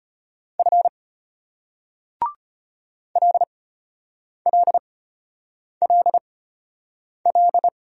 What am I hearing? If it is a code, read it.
Morse code, heard as a keyed tone, about 34 wpm, spaced as for 8 wpm, 700 Hz. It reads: F LLLL